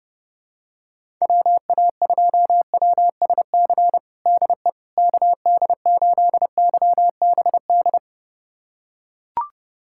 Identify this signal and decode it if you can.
Morse code, heard as a keyed tone, about 30 wpm, 700 Hz. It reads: WA2WSC DE KD8YBD
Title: USA Callsign DE Callsign - 6 Characters - 30wpm